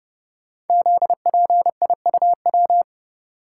Morse code, heard as a keyed tone, about 30 wpm, 700 Hz. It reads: ZPIUW
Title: Code Groups - 5 Letters - 30wpm